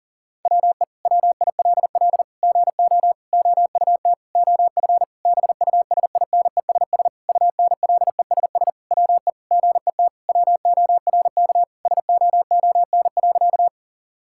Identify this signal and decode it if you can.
Morse code, heard as a keyed tone, about 40 wpm, 690 Hz. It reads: WE WILL GO OUT OF BUSINESS UNLESS WE GET WORK SOON.
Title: Sentences from Top 1000 Words - Review Part 6 - 40wpm